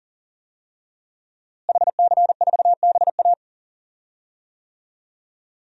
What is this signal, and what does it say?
Morse code, heard as a keyed tone, about 40 wpm, 700 Hz. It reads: HC4BA
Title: DXCC-100 Call Signs - 5 Characters - Repeated 3x - 40wpm